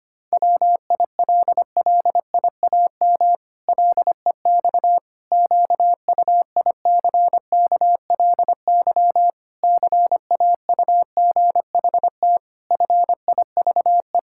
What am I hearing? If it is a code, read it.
Morse code, heard as a keyed tone, about 25 wpm, 700 Hz. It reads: WILLIAM LEX QUICKLY CAUGHT FIVE
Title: Pangram Challenge v2 - 25wpm